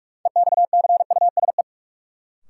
Morse code, heard as a keyed tone, about 45 wpm, 700 Hz. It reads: EXCUSE
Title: Sentences from Top 700 Words - New Word Drill - 45wpm